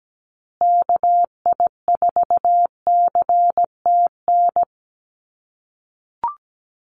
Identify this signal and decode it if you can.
Morse code, heard as a keyed tone, about 17 wpm, 700 Hz. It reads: KI4CTN
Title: USA Call Signs Repeated 2x 17wpm